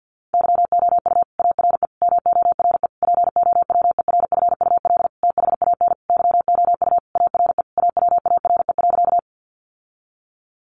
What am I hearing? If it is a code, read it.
Morse code, heard as a keyed tone, about 50 wpm, 700 Hz. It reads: YOU ARE MORE POWERFUL THAN YOU ARE AWARE.
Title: Sentences from Top 1300 Words 50wpm